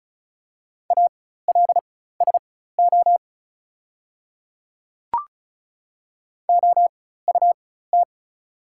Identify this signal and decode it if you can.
Morse code, heard as a keyed tone, about 36 wpm, spaced as for 16 wpm, 690 Hz. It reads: ALSO OUT
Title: 4x Character Spacing - Top 200 Words - 35wpm